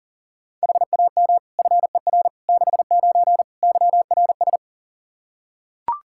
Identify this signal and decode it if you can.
Morse code, heard as a keyed tone, about 40 wpm, 700 Hz. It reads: HAM FER 69 YRS
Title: K1USN SST QSO Practice 40wpm